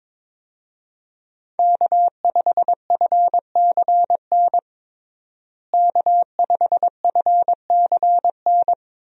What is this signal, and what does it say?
Morse code, heard as a keyed tone, about 22 wpm, 700 Hz. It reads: K5FCN K5FCN
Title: USA Call Signs - 5 Characters - Repeated 3x - 22wpm